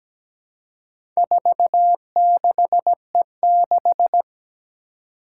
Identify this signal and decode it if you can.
Morse code, heard as a keyed tone, about 17 wpm, 700 Hz. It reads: 46E6
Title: Four Challenging Letter-Number Combinations 17wpm